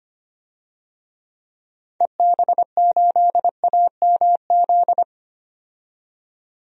Morse code, heard as a keyed tone, about 25 wpm, 700 Hz. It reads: EB8AMZ
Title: DXCC-100 Call Signs - 6 Characters - 25wpm